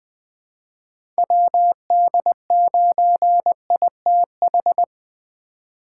Morse code, heard as a keyed tone, about 21 wpm, 700 Hz. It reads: WD9ITH
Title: USA Call Signs - 6 Characters - 4x Character Spacing - 20wpm